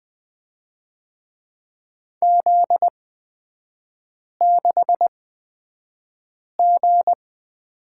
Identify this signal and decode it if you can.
Morse code, heard as a keyed tone, about 20 wpm, spaced as for 5 wpm, 700 Hz. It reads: Z6G
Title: 250 Lesson 40e - BK - Three Characters 20wpm